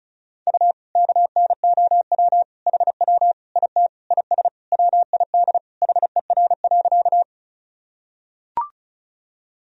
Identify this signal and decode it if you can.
Morse code, heard as a keyed tone, about 34 wpm, 700 Hz. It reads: U KNOW HW IT IS WID HER.
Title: Sentences from Top 100 Words and Abbreviations 35wpm